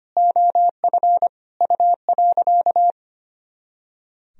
Morse code, heard as a keyed tone, about 25 wpm, 700 Hz. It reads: OF U.
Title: Sentences from Top 400 Words and Abbreviations 25wpm